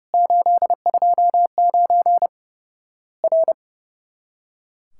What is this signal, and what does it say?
Morse code, heard as a keyed tone, about 30 wpm, 700 Hz. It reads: 829 R
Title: Worked All Europe DX Contest - QTC Exchange - 30wpm